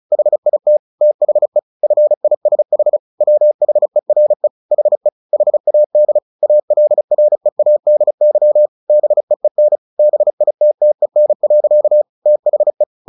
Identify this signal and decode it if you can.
Morse code, heard as a keyed tone, about 34 wpm, 600 Hz. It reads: HIT THE FISH WHERE HE HAD ALREADY BEEN BITTEN. THE
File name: the old man and the sea 2 35wpm